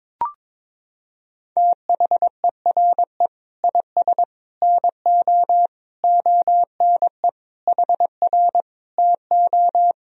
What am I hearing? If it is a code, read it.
Morse code, heard as a keyed tone, about 22 wpm, 700 Hz. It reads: THERE IS NO ONE HR TO
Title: Sentences from Top 200 Words and Abbreviations 22wpm